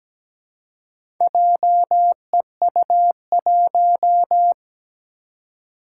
Morse code, heard as a keyed tone, about 17 wpm, 700 Hz. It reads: JEU1
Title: Four Challenging Letter-Number Combinations 17wpm